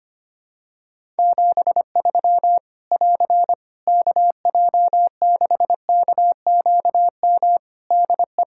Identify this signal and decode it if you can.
Morse code, heard as a keyed tone, about 25 wpm, 700 Hz. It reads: 73 <AR> KJ6KQM DE
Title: Practice QSO - Exchange 4 - Ending - 25wpm